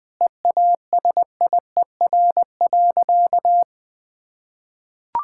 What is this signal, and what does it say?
Morse code, heard as a keyed tone, about 20 wpm, 700 Hz. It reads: EASIER.
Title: Sentences with 4-Word Phrases - Spoken Beginning - 20wpm